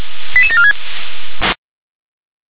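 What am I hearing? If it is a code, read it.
Five-tone selective call (ZVEI), heard as quick runs of five tones, tones of about 70 ms. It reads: 88746
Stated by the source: ZVEI five-tone selective call received on an SDR.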